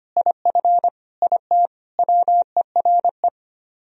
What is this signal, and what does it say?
Morse code, heard as a keyed tone, about 26 wpm, 700 Hz. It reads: IF IT WERE
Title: Speed-Racing - Sets of 4 Words 25wpm